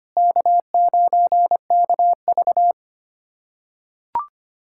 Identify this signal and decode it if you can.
Morse code, heard as a keyed tone, about 25 wpm, 700 Hz. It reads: K9KV
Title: USA Callsign DE Callsign - 4 Characters - 25wpm